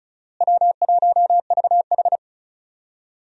Morse code, heard as a keyed tone, about 36 wpm, 700 Hz. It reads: W1VH
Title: USA Call Signs - 4 Characters - Repeated 3x - 35wpm